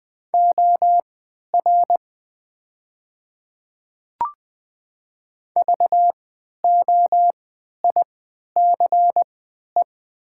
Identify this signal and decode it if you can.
Morse code, heard as a keyed tone, about 20 wpm, spaced as for 11 wpm, 700 Hz. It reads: OR VOICE
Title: 3x Character Spacing - Top 500 Words - 20wpm